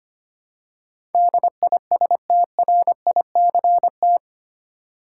Morse code, heard as a keyed tone, about 25 wpm, 700 Hz. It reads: DISTRICT
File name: Eight Letter Words - 25wpm